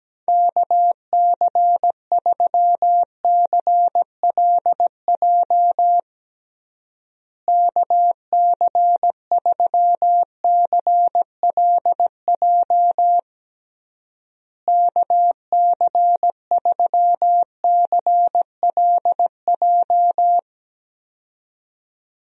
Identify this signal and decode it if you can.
Morse code, heard as a keyed tone, about 17 wpm, 700 Hz. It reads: KC3CLJ KC3CLJ KC3CLJ
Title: USA Call Signs - 6 Characters - Repeated 3x - 17wpm